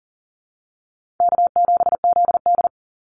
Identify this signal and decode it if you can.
Morse code, heard as a keyed tone, about 40 wpm, 700 Hz. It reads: K7ZD
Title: Easy Call Signs - 40wpm